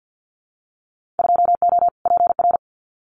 Morse code, heard as a keyed tone, about 50 wpm, 700 Hz. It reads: 2O PR
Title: Winter Field Day Exchanges Encore 50wpm